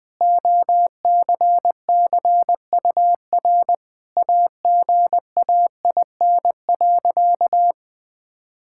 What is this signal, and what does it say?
Morse code, heard as a keyed tone, about 20 wpm, 700 Hz. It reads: OCCUR AGAIN.
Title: Sentences from Top 1000 Words - Review Part 6 - 20wpm